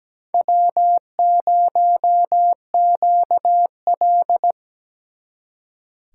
Morse code, heard as a keyed tone, about 20 wpm, 700 Hz. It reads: W0QL
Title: Speed Racing - USA 1x2 Call Signs 17wpm